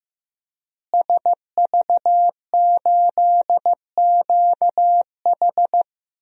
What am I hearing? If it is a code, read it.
Morse code, heard as a keyed tone, about 15 wpm, 700 Hz. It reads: SV8QH
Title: DXCC-100 Call Signs - 5 Characters - Repeated 2x - 15wpm